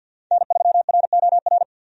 Morse code, heard as a keyed tone, about 50 wpm, 700 Hz. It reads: N3ROR
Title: USA Call Signs - 5 Characters - Repeated 2x - 50wpm